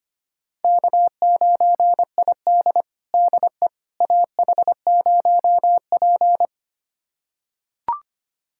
Morse code, heard as a keyed tone, about 25 wpm, 700 Hz. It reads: K9ID DE AH0P
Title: USA Callsign DE Callsign - 4 Characters - 25wpm